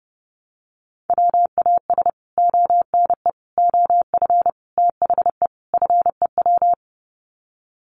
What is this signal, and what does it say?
Morse code, heard as a keyed tone, about 28 wpm, 700 Hz. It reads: WAS ONE OF THE FEW
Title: Sets of 5 Words Encore 30wpm